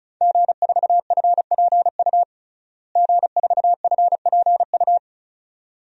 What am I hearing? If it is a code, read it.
Morse code, heard as a keyed tone, about 36 wpm, 700 Hz. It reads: G4FPU G4FPU
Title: DXCC-100 Call Signs - 5 Characters - Repeated 2x - 35wpm